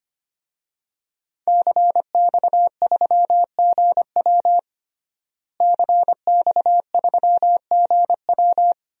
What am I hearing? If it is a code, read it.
Morse code, heard as a keyed tone, about 25 wpm, 700 Hz. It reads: CX3GW CX3GW
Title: DXCC-100 Call Signs - 5 Characters - Repeated 2x - 25wpm